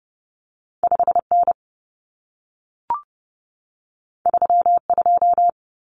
Morse code, heard as a keyed tone, about 30 wpm, 700 Hz. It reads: 5N 32
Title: Common Cut Numbers - 2-Digit 30wpm